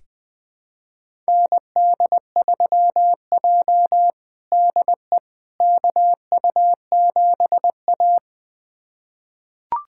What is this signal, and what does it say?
Morse code, heard as a keyed tone, about 20 wpm, 700 Hz. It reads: ND3J DE KU7A
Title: USA Callsign DE Callsign - 4 Characters - 20wpm